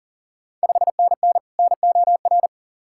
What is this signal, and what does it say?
Morse code, heard as a keyed tone, about 40 wpm, 690 Hz. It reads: 5NN NOR
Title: Ontario QSO Party Exchange 40wpm